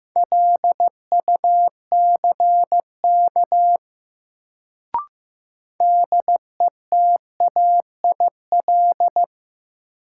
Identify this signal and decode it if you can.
Morse code, heard as a keyed tone, about 15 wpm, 700 Hz. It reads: LUCK DETAIL
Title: Sentences from Top 1500 Words - New Word Drill - 15wpm